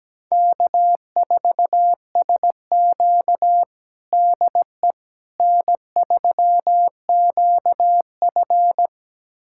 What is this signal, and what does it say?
Morse code, heard as a keyed tone, about 17 wpm, 700 Hz. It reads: K4SQ DE N3QF
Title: USA Callsign DE Callsign - 4 Characters - 17wpm